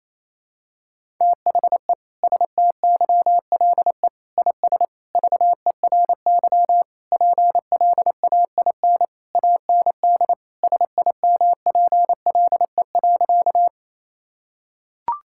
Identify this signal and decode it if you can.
Morse code, heard as a keyed tone, about 28 wpm, 700 Hz. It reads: THE STYLE IS VERY PLAIN AND SIMPLE.
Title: Sentences from Top 1000 Words - Review Part 5 - 28wpm